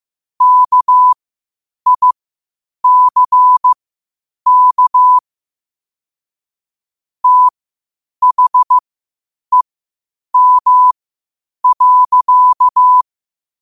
Morse code, heard as a keyed tone, about 15 wpm, spaced as for 8 wpm, 1000 Hz. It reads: KICK THEM.
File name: qotdfiles / 8wpm